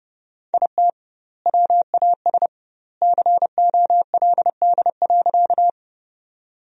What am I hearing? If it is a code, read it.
Morse code, heard as a keyed tone, about 30 wpm, 700 Hz. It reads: IT WAS COLD.
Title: 2x Word-Spacing - Sentences from Top 400 Words - 30wpm